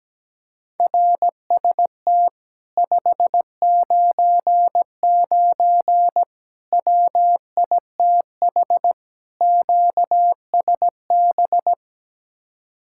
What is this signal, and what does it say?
Morse code, heard as a keyed tone, about 17 wpm, 700 Hz. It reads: RST 599 WITH QSB
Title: Speed-Racing Common QSO Phrases Encore - 17wpm